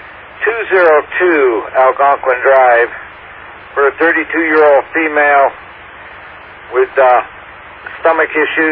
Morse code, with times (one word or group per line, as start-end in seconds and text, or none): none